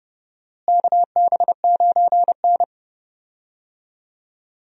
Morse code, heard as a keyed tone, about 29 wpm, 700 Hz. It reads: KB9N
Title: USA Call Signs - 4 Characters - Repeated 2x - 30wpm